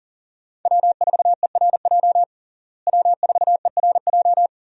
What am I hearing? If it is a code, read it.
Morse code, heard as a keyed tone, about 40 wpm, 700 Hz. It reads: W4ERJ W4ERJ
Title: USA Call Signs - 5 Characters - Repeated 3x - 40wpm